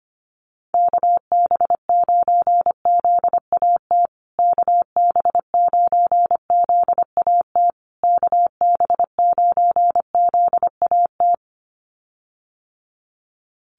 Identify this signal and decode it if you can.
Morse code, heard as a keyed tone, about 25 wpm, 700 Hz. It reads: KB9ZAT KB9ZAT KB9ZAT
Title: USA Call Signs Repeated 3x 25wpm